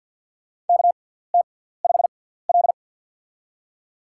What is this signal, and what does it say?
Morse code, heard as a keyed tone, about 51 wpm, spaced as for 17 wpm, 690 Hz. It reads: KT5L
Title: USA Call Signs - 4 Characters - 6x Character Spacing - 50wpm